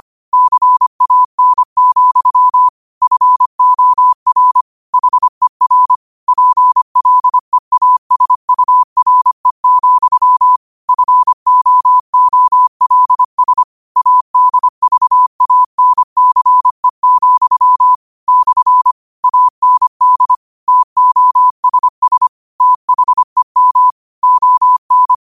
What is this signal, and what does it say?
Morse code, heard as a keyed tone, about 25 wpm, 1000 Hz. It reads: CAN, FOR HER PLEASURE, FOOLS ADVANCE, / AND TOSS THEM ON
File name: qotdfiles / 25wpm